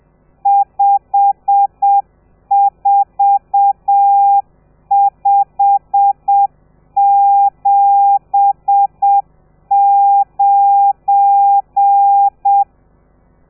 Morse code, heard as a keyed tone, about 7 wpm, 800 Hz. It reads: 54579
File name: telegrafia / i2tfj / Lez12 30